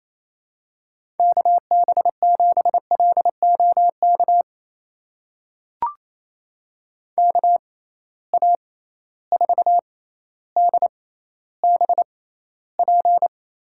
Morse code, heard as a keyed tone, about 28 wpm, spaced as for 14 wpm, 700 Hz. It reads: KB7LOK KA4DBP
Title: USA Call Signs - 6 Characters - 6x Character Spacing - 28wpm